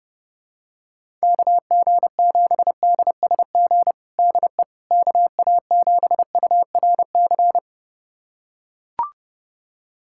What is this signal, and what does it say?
Morse code, heard as a keyed tone, about 32 wpm, 700 Hz. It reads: KG7DSG DE KA7URC
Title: USA Callsign DE Callsign - 6 Characters - 30wpm